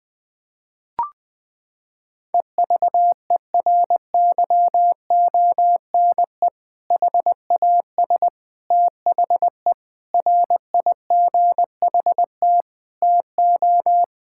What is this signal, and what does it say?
Morse code, heard as a keyed tone, about 20 wpm, 700 Hz. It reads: EVERYONE HAS THE RIGHT TO
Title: Sentences from Top 1000 Words - Review Part 5 - 20wpm